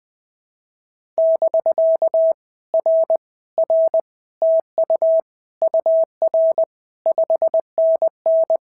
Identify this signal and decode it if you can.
Morse code, heard as a keyed tone, about 20 wpm, 650 Hz. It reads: <BK> R R TU UR 5NN
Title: POTA QSO Practice - 20wpm